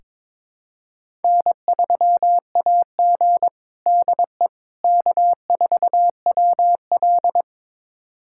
Morse code, heard as a keyed tone, about 22 wpm, 700 Hz. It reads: N3AG DE K4WL
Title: USA Callsign DE Callsign - 4 Characters - 22wpm